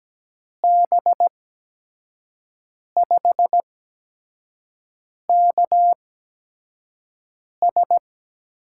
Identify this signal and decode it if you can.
Morse code, heard as a keyed tone, about 17 wpm, spaced as for 5 wpm, 700 Hz. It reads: B5KS
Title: USA Call Signs - 6 Characters - 8x Character Spacing - 17wpm